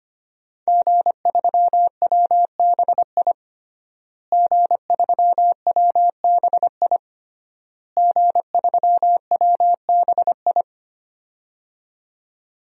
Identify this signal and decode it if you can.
Morse code, heard as a keyed tone, about 25 wpm, 700 Hz. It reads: G3WBI G3WBI G3WBI
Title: DXCC-100 Call Signs - 5 Characters - Repeated 3x - 25wpm